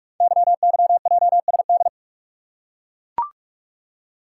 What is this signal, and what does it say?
Morse code, heard as a keyed tone, about 44 wpm, 700 Hz. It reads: YYJSD
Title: Code Groups - 5 Letters - 45wpm